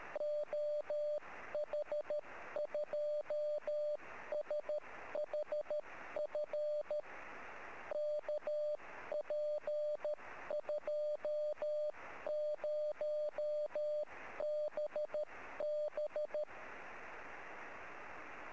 Morse code, heard as a keyed tone, about 13 wpm, 600 Hz. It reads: OH2SHF KP20BB
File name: sm7lcb / oh2shf 23-13-3cm